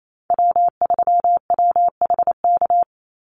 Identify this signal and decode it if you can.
Morse code, heard as a keyed tone, about 28 wpm, 700 Hz. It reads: W3WHK
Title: Call Signs - 28wpm